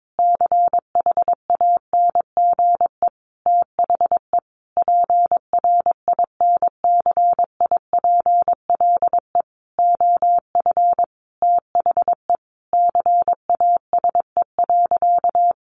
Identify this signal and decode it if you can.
Morse code, heard as a keyed tone, about 22 wpm, 700 Hz. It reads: CHANGE THE PRINCIPLE OF THE CASE.
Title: Sentences from Top 1400 Words 22wpm